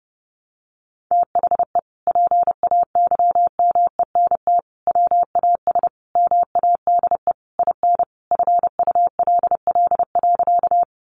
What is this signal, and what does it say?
Morse code, heard as a keyed tone, about 28 wpm, 700 Hz. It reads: THE PAYMENT WAS MADE IN FULL.
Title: Sentences from Top 1200 Words 30wpm